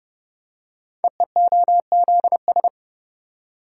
Morse code, heard as a keyed tone, about 30 wpm, 700 Hz. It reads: EEOZS
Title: Code Groups - 5 Letters - 30wpm